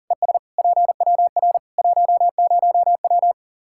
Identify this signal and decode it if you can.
Morse code, heard as a keyed tone, about 40 wpm, 700 Hz. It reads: ES PWR 10W
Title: Practice QSO - Exchange 2 - Info - 40wpm